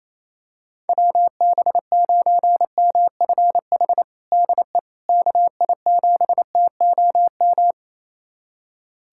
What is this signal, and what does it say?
Morse code, heard as a keyed tone, about 27 wpm, 700 Hz. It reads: WB9MFH DE KI7TOM
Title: USA Callsign DE Callsign - 6 Characters - 28wpm